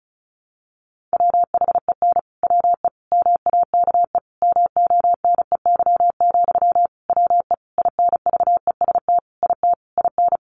Morse code, heard as a keyed tone, about 34 wpm, 700 Hz. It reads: WHEN WE MAKE MONEY, WE INVEST IT IN
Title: Sentences from Top 1100 Words 35wpm